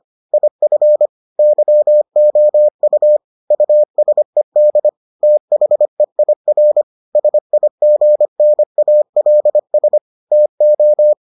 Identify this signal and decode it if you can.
Morse code, heard as a keyed tone, about 25 wpm, 600 Hz. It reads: IF YOU USED THEIR SIGNALS TO